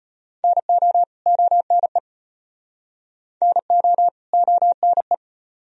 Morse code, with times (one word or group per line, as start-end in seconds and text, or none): 0.4-1.0 NO
1.3-2.0 ONE
3.4-4.1 NO
4.3-5.1 ONE